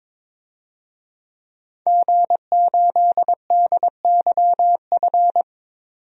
Morse code, heard as a keyed tone, about 22 wpm, 700 Hz. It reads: G8DYF
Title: DXCC-100 Call Signs - 5 Characters - 22wpm